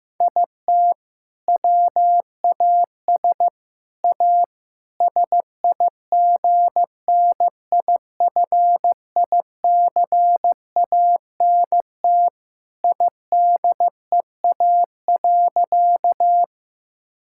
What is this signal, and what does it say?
Morse code, heard as a keyed tone, about 16 wpm, 700 Hz. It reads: IT WAS A SIGNIFICANT IDEA.
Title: Sentences from Top 1000 Words - Review Part 6 - 15wpm